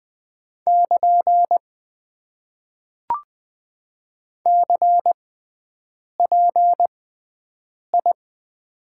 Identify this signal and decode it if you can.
Morse code, heard as a keyed tone, about 20 wpm, spaced as for 7 wpm, 700 Hz. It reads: <KN> CPI
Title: Top 90 QSO Elements - 6x Character Spacing - 20wpm